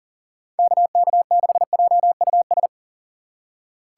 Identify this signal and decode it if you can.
Morse code, heard as a keyed tone, about 40 wpm, 700 Hz. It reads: KK6JUS
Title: USA Call Signs - 2x Character Spacing - 40wpm